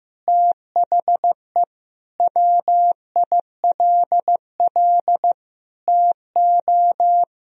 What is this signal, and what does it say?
Morse code, heard as a keyed tone, about 15 wpm, 700 Hz. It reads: THE WILL TO
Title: Sentences from Top 1000 Words - Review Part 1 - 15wpm